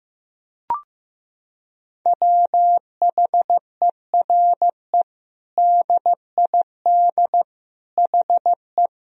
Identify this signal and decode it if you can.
Morse code, heard as a keyed tone, about 15 wpm, 700 Hz. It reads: WHERE DID HE
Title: Sentences from Top 200 Words and Abbreviations 15wpm